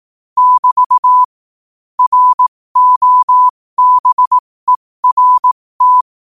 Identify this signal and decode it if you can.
Morse code, heard as a keyed tone, about 18 wpm, spaced as for 15 wpm, 1000 Hz. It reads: <BT> ROBERT
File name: 15wpm